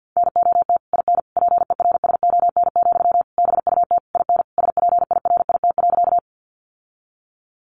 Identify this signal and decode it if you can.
Morse code, heard as a keyed tone, about 50 wpm, 700 Hz. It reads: NOT IN PERSON, BUT IN SPIRIT.
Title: Sentences from Top 1400 Words 50wpm